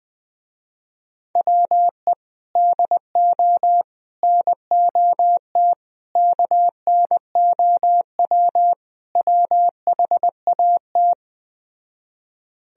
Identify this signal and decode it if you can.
Morse code, heard as a keyed tone, about 20 wpm, 700 Hz. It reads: WE DO NOT KNOW WHAT
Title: Speed-Racing - Sets of 5 Words 17wpm